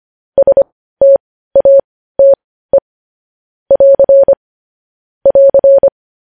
Morse code, heard as a keyed tone, about 25 wpm, spaced as for 15 wpm, 550 Hz. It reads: STATE <AR> <AR>
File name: podcast / News headlines-1x-25x15